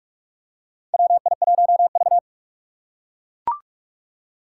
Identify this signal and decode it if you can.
Morse code, heard as a keyed tone, about 44 wpm, 700 Hz. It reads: WI1V
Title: USA Call Signs - 4 Characters - 6x Character Spacing - 45wpm